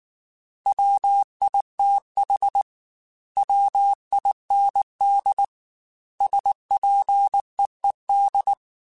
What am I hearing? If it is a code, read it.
Morse code, heard as a keyed tone, about 19 wpm, 800 Hz. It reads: WITH WIND SPEED
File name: Weather Sets / Weather Sets-mp3s / Weather 5 1 @18